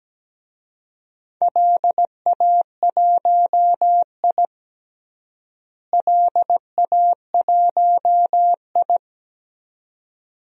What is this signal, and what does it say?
Morse code, heard as a keyed tone, about 17 wpm, 700 Hz. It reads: LA1I LA1I
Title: DXCC-100 Call Signs - 4 Characters - Repeated 2x - 17wpm